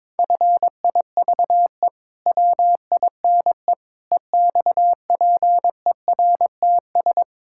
Morse code, heard as a keyed tone, about 22 wpm, 700 Hz. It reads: FIVE WINE EXPERTS
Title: Pangram Challenge v2 - 22wpm